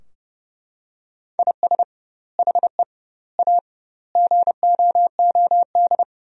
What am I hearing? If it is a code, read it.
Morse code, heard as a keyed tone, about 31 wpm, 700 Hz. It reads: IS HE A GOOD
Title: 2x Word-Spacing - Sentences from Top 200 Words - 30wpm